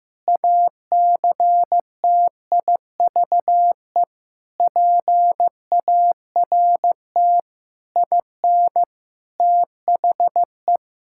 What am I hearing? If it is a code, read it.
Morse code, heard as a keyed tone, about 15 wpm, 700 Hz. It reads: ACTIVE PART IN THE
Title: Sentences from Top 1000 Words - Review Part 6 - 15wpm